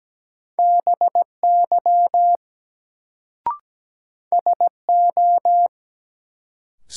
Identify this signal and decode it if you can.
Morse code, heard as a keyed tone, about 17 wpm, 700 Hz. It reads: BY SO
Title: Sentences from Top 100 Words - New Word Drill - 17wpm